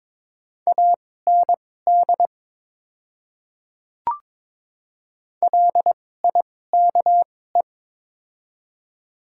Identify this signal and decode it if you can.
Morse code, heard as a keyed tone, about 22 wpm, spaced as for 16 wpm, 690 Hz. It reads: AND LIKE
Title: 2x Character Spacing - Top 100 Words - 22wpm